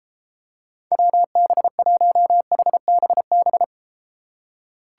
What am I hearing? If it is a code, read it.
Morse code, heard as a keyed tone, about 34 wpm, 700 Hz. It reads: WB1HBB